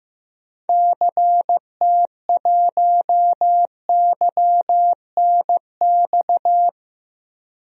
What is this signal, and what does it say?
Morse code, heard as a keyed tone, about 15 wpm, 700 Hz. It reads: CT1YNX
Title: DXCC-100 Call Signs - 6 Characters - Repeated 3x - 15wpm